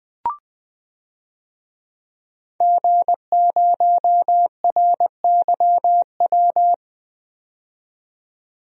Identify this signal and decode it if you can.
Morse code, heard as a keyed tone, about 20 wpm, 700 Hz. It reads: G0RYW
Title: DXCC-100 Call Signs - 5 Characters - 20wpm